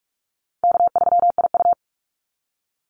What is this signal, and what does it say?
Morse code, heard as a keyed tone, about 44 wpm, 700 Hz. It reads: K3IU
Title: Easy Call Signs - 45wpm